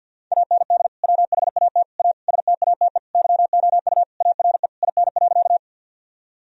Morse code, heard as a keyed tone, about 50 wpm, 700 Hz. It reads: AND WHAT A STATE YOU ARE IN.
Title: Sentences from Top 1000 Words - Review Part 2 - 50wpm